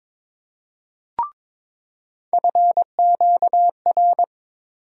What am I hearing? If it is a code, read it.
Morse code, heard as a keyed tone, about 22 wpm, 700 Hz. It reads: FQR
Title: USA Call Sign Postfixes - 3 Letters 22wpm